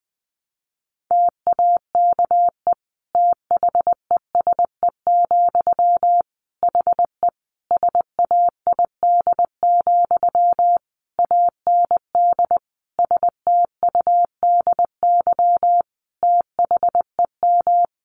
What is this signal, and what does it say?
Morse code, heard as a keyed tone, about 20 wpm, 700 Hz. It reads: TAKE THESE, HE SAID, AND STUDY THEM
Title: Sentences from Top 300 Words 20wpm